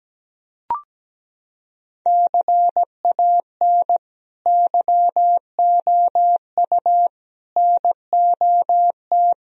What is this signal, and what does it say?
Morse code, heard as a keyed tone, about 17 wpm, 700 Hz. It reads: CAN YOU NOT
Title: Sentences from Top 1000 Words - Review Part 2 - 17wpm